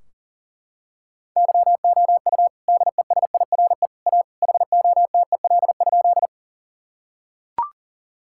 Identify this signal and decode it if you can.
Morse code, heard as a keyed tone, about 40 wpm, 700 Hz. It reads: YOU DESIRE A HOTEL?
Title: Sentences from Top 1600 Words 40wpm